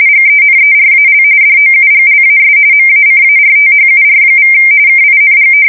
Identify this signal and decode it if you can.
RTTY, about 45.45 baud. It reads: GENERAL DOUGLAS MACARTHUR= KING
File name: fsk170 / usnavy4